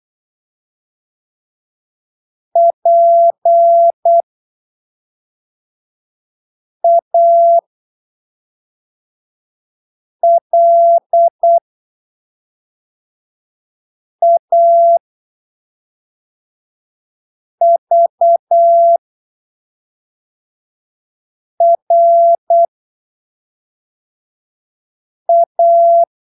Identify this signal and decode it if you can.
Morse code, heard as a keyed tone, about 8 wpm, spaced as for 3 wpm, 680 Hz. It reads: PALAVRA
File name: arquivos / texto / texto 25 08ppm